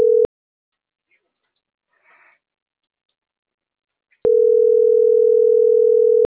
Ringback tone (North America), 440 + 480 Hz.